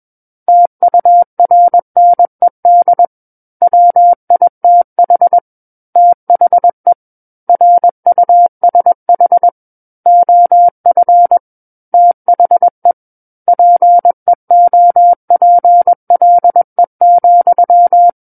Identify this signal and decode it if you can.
Morse code, heard as a keyed tone, about 21 wpm, 700 Hz. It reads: TURNED WITH THE RUSH OF THE PEOPLE,
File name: Book One / 12-Chapter 12 - 21 WPM